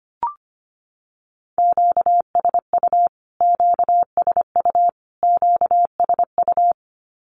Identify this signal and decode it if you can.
Morse code, heard as a keyed tone, about 25 wpm, 700 Hz. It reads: QSU QSU QSU
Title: Q-Codes Repeated 3x 25wpm